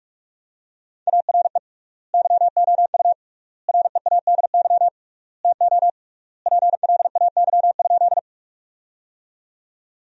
Morse code, heard as a keyed tone, about 46 wpm, 700 Hz. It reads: ARE YOU READY TO PLAY?
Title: 3x Word-Spacing - Sentences from Top 400 Words - 45wpm